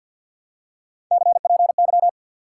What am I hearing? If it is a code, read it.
Morse code, heard as a keyed tone, about 53 wpm, 700 Hz. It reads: CPY
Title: Top 180 QSO Elements 50wpm